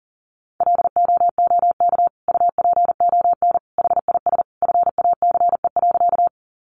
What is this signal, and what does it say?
Morse code, heard as a keyed tone, about 40 wpm, 700 Hz. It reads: LOOK UPON HIS FACE.
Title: Sentences from Top 1100 Words 40wpm